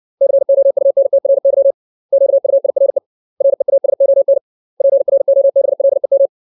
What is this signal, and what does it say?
Morse code, heard as a keyed tone, about 60 wpm, 550 Hz. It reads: COUNTRY QUIRE REASON PROBLEM
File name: 632 words / 632 Words 60WPM